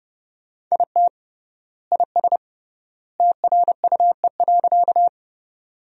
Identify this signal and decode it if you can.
Morse code, heard as a keyed tone, about 30 wpm, 700 Hz. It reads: IT IS TRUE.
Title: 3x Word-Spacing - Sentences from Top 400 Words - 30wpm